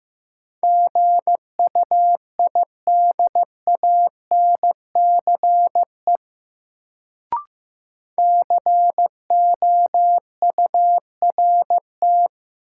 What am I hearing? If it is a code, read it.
Morse code, heard as a keyed tone, about 15 wpm, 700 Hz. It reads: GUIDANCE COURT
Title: Sentences from Top 1600 Words - New Word Drill - 15wpm